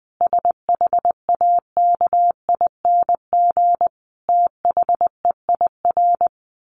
Morse code, heard as a keyed tone, about 20 wpm, 700 Hz. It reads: SHAKING THEIR
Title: A Sound Of Thunder 20wpm